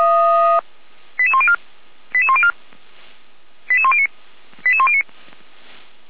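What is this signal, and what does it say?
Five-tone selective call (ZVEI), heard as quick runs of five tones, tones of about 70 ms. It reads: 88104 88104 88109 88109
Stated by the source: ZVEI five-tone call received by SDR at feuerwehr-orlamuende.org, fire brigade Orlamünde.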